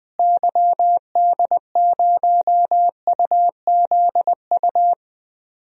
Morse code, heard as a keyed tone, about 21 wpm, 700 Hz. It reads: YD0UZU
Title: DXCC-100 Call Signs - 6 Characters - Repeated 3x - 20wpm